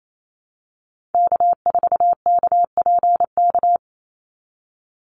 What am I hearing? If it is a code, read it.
Morse code, heard as a keyed tone, about 28 wpm, 700 Hz. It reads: K4KPK
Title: NA SOTA Activator Callsigns 28wpm